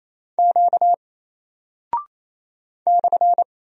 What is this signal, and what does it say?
Morse code, heard as a keyed tone, about 27 wpm, 700 Hz. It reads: Q /
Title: Single Character (Letter:Number:Punctuation:Prosign) 28wpm